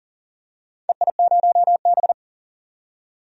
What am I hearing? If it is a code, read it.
Morse code, heard as a keyed tone, about 38 wpm, 700 Hz. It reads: EI0B
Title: DXCC-100 Call Signs - 4 Characters - 40wpm